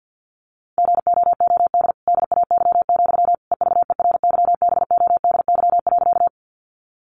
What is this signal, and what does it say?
Morse code, heard as a keyed tone, about 50 wpm, 700 Hz. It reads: GOOD DAY, EVERYBODY.
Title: Sentences from Top 1400 Words 50wpm